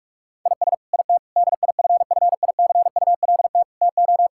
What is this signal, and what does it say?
Morse code, heard as a keyed tone, about 46 wpm, 700 Hz. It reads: IS IT DIFFICULT TO
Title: Sentences from Top 1000 Words - Review Part 6 - 45wpm